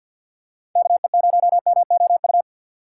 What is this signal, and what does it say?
Morse code, heard as a keyed tone, about 47 wpm, 700 Hz. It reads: KE0MOU
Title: USA Call Signs - 3x Character Spacing - 50wpm